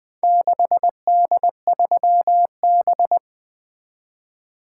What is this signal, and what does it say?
Morse code, heard as a keyed tone, about 23 wpm, 700 Hz. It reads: 6D3B
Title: Speed-Racing - Four Letter-Number Combinations - 20wpm